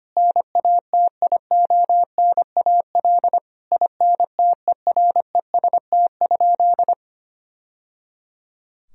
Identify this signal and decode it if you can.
Morse code, heard as a keyed tone, about 25 wpm, 700 Hz. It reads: NATIONAL INTEREST?
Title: Sentences from Top 1000 Words - Review Part 4 - 25wpm